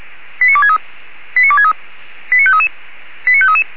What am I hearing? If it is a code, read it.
Five-tone selective call (ZVEI), heard as quick runs of five tones, tones of about 70 ms. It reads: 89263 89263 89533 89533